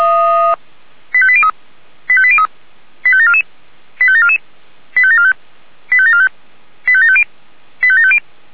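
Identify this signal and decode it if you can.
Five-tone selective call (ZVEI), heard as quick runs of five tones, tones of about 70 ms. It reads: 85803 85803 85744 85744 85745 85745 85750 85750
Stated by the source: ZVEI five-tone call received by SDR.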